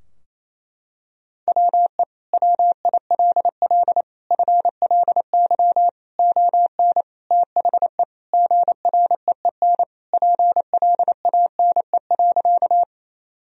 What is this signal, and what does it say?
Morse code, heard as a keyed tone, about 28 wpm, 700 Hz. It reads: WE WILL FLY ON THE GREEN PLANE.
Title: Sentences from Top 1000 Words - Review Part 3 - 28wpm